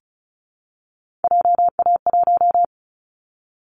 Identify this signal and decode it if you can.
Morse code, heard as a keyed tone, about 34 wpm, 700 Hz. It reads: JA1
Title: Callsign Prefixes for Top 25 DXCC - Repeated 3x - 35wpm